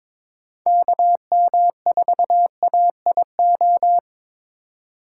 Morse code, heard as a keyed tone, about 22 wpm, 700 Hz. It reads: KM4AIO
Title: USA Call Signs - 6 Characters - 4x Character Spacing - 22wpm